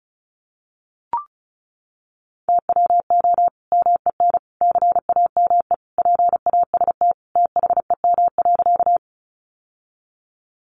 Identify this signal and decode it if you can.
Morse code, heard as a keyed tone, about 34 wpm, 700 Hz. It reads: TWO MEN CAME PAST THEM.
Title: Sentences from Top 600 Words 35wpm